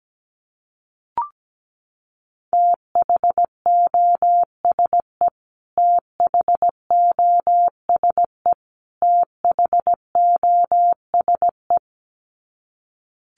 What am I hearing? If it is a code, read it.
Morse code, heard as a keyed tone, about 17 wpm, 700 Hz. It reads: THOSE THOSE THOSE
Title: Top 500 Words Repeated 17wpm